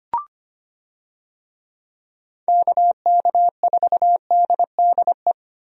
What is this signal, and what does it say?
Morse code, heard as a keyed tone, about 25 wpm, 700 Hz. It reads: KK4DDE
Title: USA Call Signs - 6 Characters - 25wpm